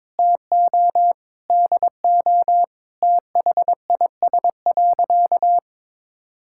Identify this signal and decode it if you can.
Morse code, heard as a keyed tone, about 22 wpm, 700 Hz. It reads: TO DO THIS.
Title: Sentences from Top 1000 Words - Review Part 5 - 22wpm